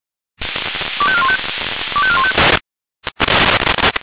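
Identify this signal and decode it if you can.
Five-tone selective call (ZVEI), heard as quick runs of five tones, tones of about 70 ms. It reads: 25415 25415
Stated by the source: ZVEI five-tone selective call received on an SDR.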